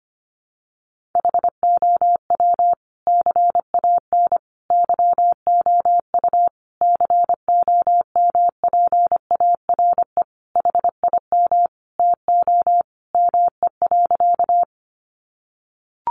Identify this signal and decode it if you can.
Morse code, heard as a keyed tone, about 25 wpm, 700 Hz. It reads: HOW CAN YOU COMPARE HIM TO ME.
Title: Sentences from Top 1300 Words 25wpm